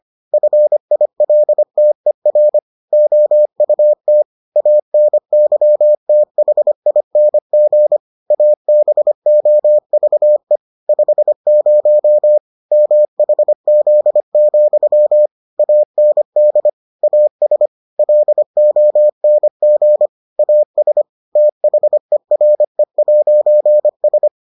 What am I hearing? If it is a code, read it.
Morse code, heard as a keyed tone, about 25 wpm, 600 Hz. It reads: FILTER OUT ANYTHING ABOVE 50 MHZ, AND AS LONG AS THERE'S